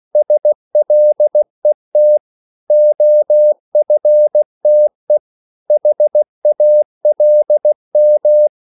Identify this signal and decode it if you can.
Morse code, heard as a keyed tone, about 16 wpm, 600 Hz. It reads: SLET OFTE HALM
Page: Korte danske ord 16wpm